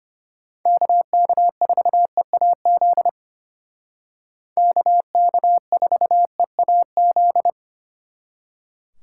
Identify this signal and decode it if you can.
Morse code, heard as a keyed tone, about 33 wpm, 700 Hz. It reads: KK4EAZ KK4EAZ